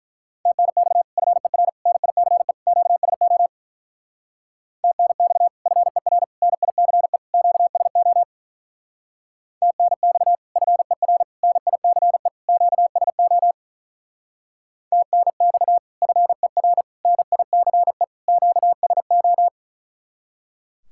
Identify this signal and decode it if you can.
Morse code, heard as a keyed tone, about 42 wpm, 700 Hz. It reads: TNX FER NICE QSO TNX FER NICE QSO TNX FER NICE QSO TNX FER NICE QSO
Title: Speed-Racing Common QSO Phrases Encore - 35wpm